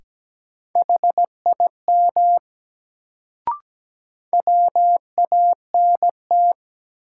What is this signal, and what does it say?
Morse code, heard as a keyed tone, about 17 wpm, 700 Hz. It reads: HIM WANT
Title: Sets of 2 Words - New Word Drill - 17wpm